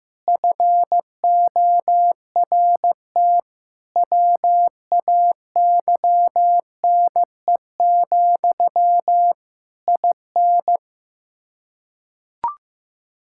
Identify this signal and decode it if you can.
Morse code, heard as a keyed tone, about 15 wpm, 700 Hz. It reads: FORT WAYNE, IN
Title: Practice QSO - Exchange 1 - Beginning - 15wpm